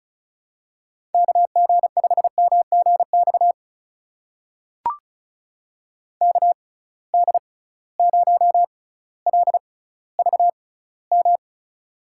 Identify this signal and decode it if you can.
Morse code, heard as a keyed tone, about 36 wpm, spaced as for 18 wpm, 700 Hz. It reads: KG5MGX KD0LVM
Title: USA Call Signs - 6 Characters - 6x Character Spacing - 35wpm